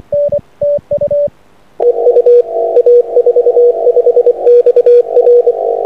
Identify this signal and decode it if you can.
Morse code, heard as a keyed tone, about 24 wpm, 480 Hz, with radio noise. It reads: EU A45XR
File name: a45xr-6m